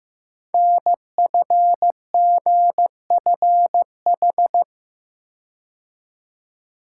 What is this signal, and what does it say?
Morse code, heard as a keyed tone, about 15 wpm, 700 Hz. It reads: NFGFH
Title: Code Groups - 5 Letters - 15wpm